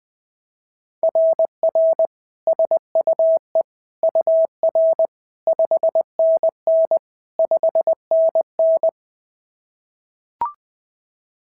Morse code, heard as a keyed tone, about 20 wpm, 690 Hz. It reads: RR SUE UR 5NN 5NN
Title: SOTA QSO Practice 20wpm